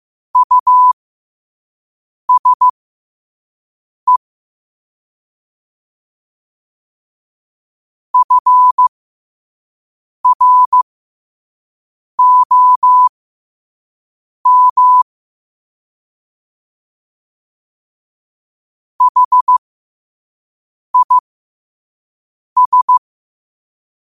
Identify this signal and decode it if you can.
Morse code, heard as a keyed tone, about 15 wpm, spaced as for 5 wpm, 1000 Hz. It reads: USE FROM HIS
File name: qotdfiles / 5wpm